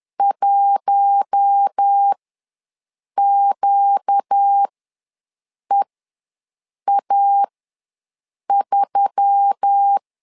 Morse code, heard as a keyed tone, about 10 wpm, spaced as for 6 wpm, 800 Hz. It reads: 1QEA3
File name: LEZIONI / 4 Q1S3 ws 10wpm 2437